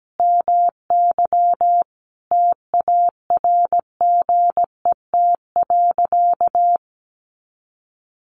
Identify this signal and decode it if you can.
Morse code, heard as a keyed tone, about 17 wpm, 700 Hz. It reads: MY TARGET.
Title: Sentences from Top 1200 Words 17wpm